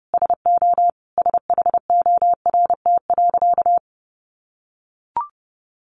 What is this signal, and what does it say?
Morse code, heard as a keyed tone, about 30 wpm, 700 Hz. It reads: SO SHORT.
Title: Sentences from Top 500 Words 30wpm